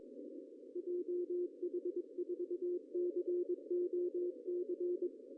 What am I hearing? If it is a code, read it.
Morse code, heard as a keyed tone, about 21 wpm, 360 Hz, with radio noise. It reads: JH4COC
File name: CW / cqww-cw-k5tr